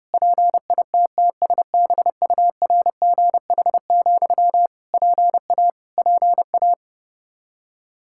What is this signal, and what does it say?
Morse code, heard as a keyed tone, about 30 wpm, 700 Hz. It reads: PITTSBURGH, PA PA
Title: Practice QSO - Exchange 1 - Beginning - 30wpm